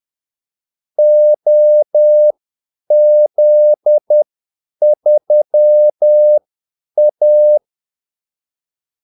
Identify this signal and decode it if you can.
Morse code, heard as a keyed tone, about 10 wpm, spaced as for 8 wpm, 600 Hz. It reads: OZ3A
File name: Callsigns-10-8wpm